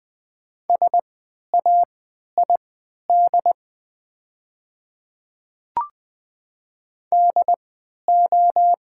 Morse code, heard as a keyed tone, about 20 wpm, spaced as for 11 wpm, 700 Hz. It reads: SAID DO
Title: 3x Character Spacing - Top 500 Words - 20wpm